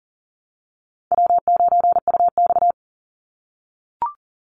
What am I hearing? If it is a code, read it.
Morse code, heard as a keyed tone, about 40 wpm, 700 Hz. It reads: W9UX
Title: NA SOTA Chaser Callsigns 40wpm